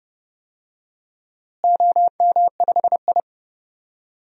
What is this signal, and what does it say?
Morse code, heard as a keyed tone, about 32 wpm, 700 Hz. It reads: OM5I
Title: DXCC-100 Call Signs - 4 Characters - Repeated 3x - 30wpm